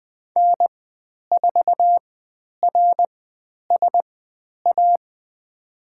Morse code, heard as a keyed tone, about 20 wpm, spaced as for 10 wpm, 700 Hz. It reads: N4RSA
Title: 102 Lesson 17g - Letter u - Call Signs 20wpm